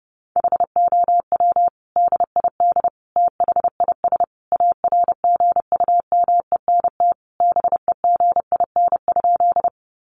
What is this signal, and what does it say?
Morse code, heard as a keyed tone, about 30 wpm, 700 Hz. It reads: HOW DID THIS ARGUMENT BEGIN?
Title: Sentences from Top 1400 Words 30wpm